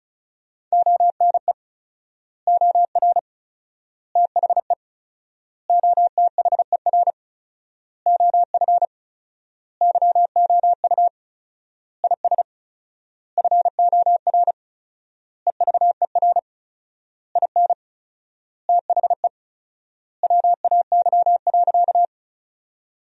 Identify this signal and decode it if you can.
Morse code, heard as a keyed tone, about 34 wpm, 700 Hz. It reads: ONE OR THE OTHER OF YOU IS FOR EVER IN THE WAY.
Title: 4x Word-Spacing - Sentences from Top 400 Words - 35wpm